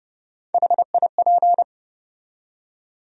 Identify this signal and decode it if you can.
Morse code, heard as a keyed tone, about 30 wpm, 700 Hz. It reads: HIP
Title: USA Call Sign Postfixes - 3 Letters 30wpm